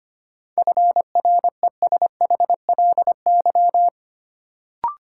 Morse code, heard as a keyed tone, about 25 wpm, 700 Hz. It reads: FRESHLY
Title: Sets of 5 Words Encore - New Word Drill - 25wpm